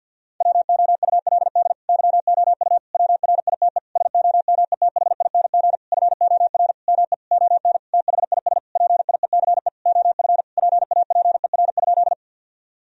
Morse code, heard as a keyed tone, about 50 wpm, 700 Hz. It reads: WOULD YOU WRITE SOMETHING FOR ME ON THIS PIECE OF PAPER?
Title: Sentences from Top 1000 Words - Review Part 2 - 50wpm